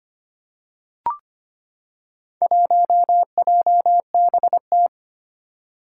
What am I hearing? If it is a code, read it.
Morse code, heard as a keyed tone, about 25 wpm, 700 Hz. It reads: 1JBT
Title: Four Challenging Letter-Number Combinations 25wpm